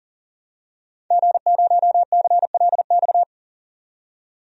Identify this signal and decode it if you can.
Morse code, heard as a keyed tone, about 40 wpm, 700 Hz. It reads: G0CLX
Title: DXCC-100 Call Signs - 5 Characters - 40wpm